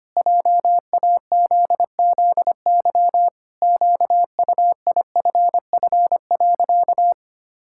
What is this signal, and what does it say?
Morse code, heard as a keyed tone, about 25 wpm, 700 Hz. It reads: JAZZY QUIFF.
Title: Pangram Challenge v2 - 25wpm